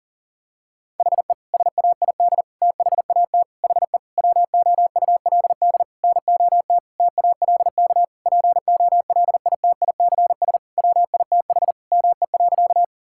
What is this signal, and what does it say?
Morse code, heard as a keyed tone, about 40 wpm, 700 Hz. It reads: HE SAID THAT HE WOULD NOT TALK POLITICS WITH ME.
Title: Sentences from Top 1000 Words - Review Part 6 - 40wpm